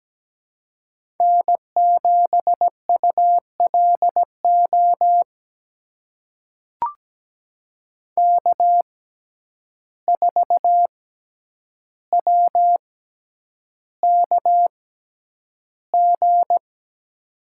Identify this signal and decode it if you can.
Morse code, heard as a keyed tone, about 17 wpm, spaced as for 9 wpm, 700 Hz. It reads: N7ULO K4WKG
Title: USA Call Signs - 5 Characters - 6x Character Spacing - 17wpm